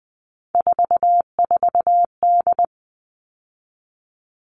Morse code, heard as a keyed tone, about 20 wpm, 700 Hz. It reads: 44D
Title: All Cut Numbers - 3-Digit 20wpm